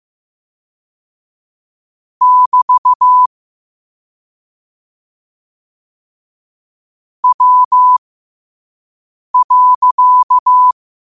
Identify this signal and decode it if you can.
Morse code, heard as a keyed tone, about 15 wpm, spaced as for 5 wpm, 1000 Hz. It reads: <BT> W.
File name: qotdfiles / 5wpm